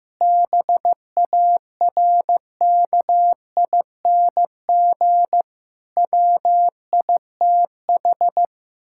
Morse code, heard as a keyed tone, about 16 wpm, 700 Hz. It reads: BARKING WITH
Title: Pangram Challenge v2 - 15wpm